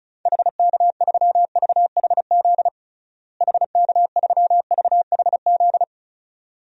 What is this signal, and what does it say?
Morse code, heard as a keyed tone, about 36 wpm, 700 Hz. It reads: HK3VHZ HK3VHZ
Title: DXCC-100 Call Signs - 6 Characters - Repeated 2x - 35wpm